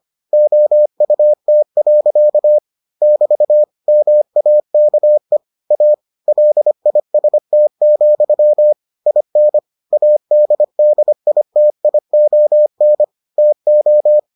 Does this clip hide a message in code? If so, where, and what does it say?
Yes: Morse code from 0.3 to 14.2 s, heard as a keyed tone, about 25 wpm, 600 Hz. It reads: OUT. <BT> MAKE A LIST, IN ADDITION TO